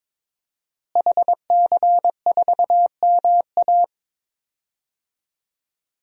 Morse code, heard as a keyed tone, about 22 wpm, 700 Hz. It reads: HC4MA
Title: DXCC-100 Call Signs - 5 Characters - Repeated 2x - 22wpm